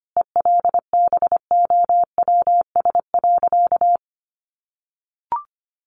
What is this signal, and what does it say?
Morse code, heard as a keyed tone, about 25 wpm, 700 Hz. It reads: ELBOWS.
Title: A Sound Of Thunder 25wpm